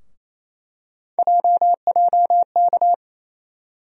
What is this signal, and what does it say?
Morse code, heard as a keyed tone, about 28 wpm, 700 Hz. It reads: JJK
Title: USA Call Sign Postfixes - 3 Letters 28wpm